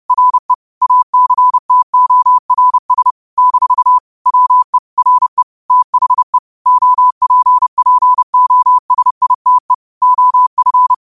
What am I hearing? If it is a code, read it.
Morse code, heard as a keyed tone, about 30 wpm, 1000 Hz. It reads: RE ACTORS - WE RE THE OPPOSITE OF
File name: qotdfiles / 30wpm